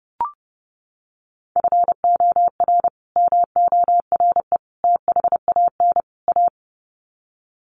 Morse code, heard as a keyed tone, about 30 wpm, 700 Hz. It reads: FOR MORE THAN A
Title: Sets of 4 Words 30wpm